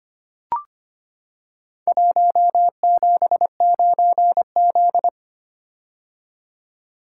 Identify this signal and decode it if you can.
Morse code, heard as a keyed tone, about 25 wpm, 700 Hz. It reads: 179Z
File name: practice / 25wpm / Four Letter-Number Combinations - 25wpm